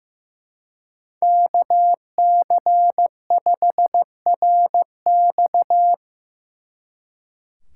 Morse code, heard as a keyed tone, about 17 wpm, 700 Hz. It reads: KC5RX
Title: Speed Racing - USA 2x2 Call Signs 15wpm